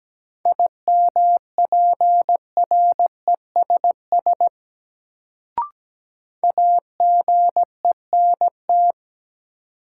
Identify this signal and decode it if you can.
Morse code, heard as a keyed tone, about 17 wpm, 700 Hz. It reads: IMPRESS AGENT
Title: Sentences from Top 1600 Words - New Word Drill - 17wpm